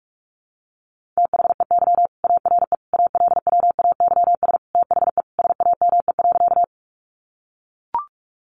Morse code, heard as a keyed tone, about 44 wpm, 700 Hz. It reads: THEY ARE ALWAYS THE SAME.
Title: Sentences from Top 300 Words 45wpm